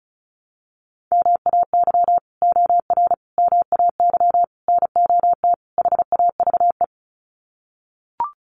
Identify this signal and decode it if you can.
Morse code, heard as a keyed tone, about 34 wpm, 700 Hz. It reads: MAY OR MAY NOT HAVE
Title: Sets of 5 Words Encore2 35wpm